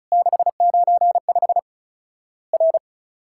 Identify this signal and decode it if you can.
Morse code, heard as a keyed tone, about 36 wpm, 700 Hz. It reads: 695 R
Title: Worked All Europe DX Contest - QTC Exchange - 35wpm